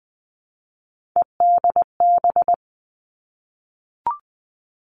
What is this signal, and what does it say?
Morse code, heard as a keyed tone, about 19 wpm, 700 Hz. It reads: EDB
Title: Three Challenging Letter-Number Combinations 20wpm